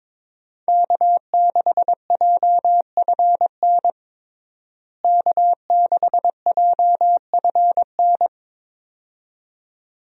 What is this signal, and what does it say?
Morse code, heard as a keyed tone, about 22 wpm, 700 Hz. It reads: K6JFN K6JFN
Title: USA Call Signs - 5 Characters - Repeated 2x - 22wpm